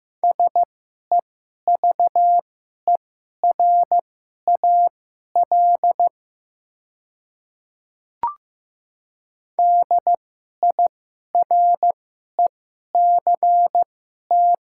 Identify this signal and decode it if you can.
Morse code, heard as a keyed tone, about 15 wpm, spaced as for 11 wpm, 700 Hz. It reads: SEVERAL DIRECT
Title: 2x Character Spacing - Top 500 Words - 15wpm